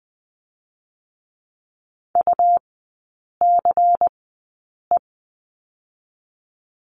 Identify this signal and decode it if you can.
Morse code, heard as a keyed tone, about 20 wpm, spaced as for 8 wpm, 700 Hz. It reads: UCE
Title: 100 Lesson 17e - Letter u - Three Characters 20wpm